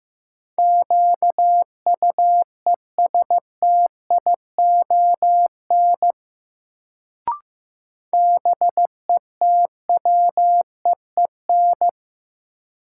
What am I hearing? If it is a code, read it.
Morse code, heard as a keyed tone, about 15 wpm, 700 Hz. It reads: QUESTION BETWEEN
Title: Sentences from Top 400 Words - New Word Drill - 15wpm